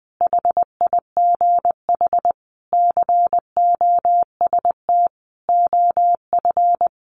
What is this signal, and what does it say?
Morse code, heard as a keyed tone, about 20 wpm, 700 Hz. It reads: HIGH COST OF
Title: Sentences from Top 1400 Words 20wpm